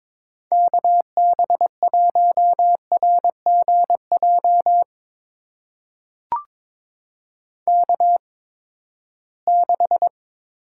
Morse code, heard as a keyed tone, about 22 wpm, spaced as for 15 wpm, 700 Hz. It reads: KB1RGJ K6
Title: USA Call Signs - 8x Character Spacing - 22wpm